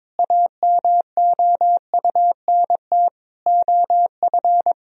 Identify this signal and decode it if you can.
Morse code, heard as a keyed tone, about 22 wpm, 700 Hz. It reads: AMOUNT OF
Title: Sentences from Top 1000 Words - Review Part 4 - 22wpm